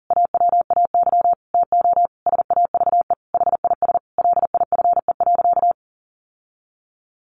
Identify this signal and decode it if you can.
Morse code, heard as a keyed tone, about 40 wpm, 700 Hz. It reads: AWAY TO SAVE HIS LIFE.
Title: Sentences from Top 1100 Words 40wpm